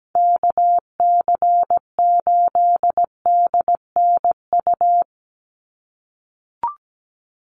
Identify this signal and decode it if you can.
Morse code, heard as a keyed tone, about 17 wpm, 700 Hz. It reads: KC8DNU
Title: USA Call Signs 17wpm